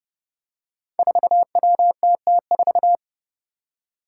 Morse code, heard as a keyed tone, about 29 wpm, 700 Hz. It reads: 4WTT4
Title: Five Challenging Letter-Number Combinations 30wpm